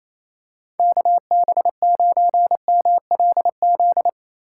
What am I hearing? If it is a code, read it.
Morse code, heard as a keyed tone, about 28 wpm, 700 Hz. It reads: KB9MLZ
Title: USA Call Signs - 6 Characters - 3x Character Spacing - 28wpm